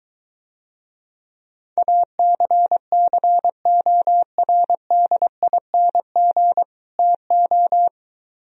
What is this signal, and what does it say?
Morse code, heard as a keyed tone, about 23 wpm, 700 Hz. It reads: ACCORDING TO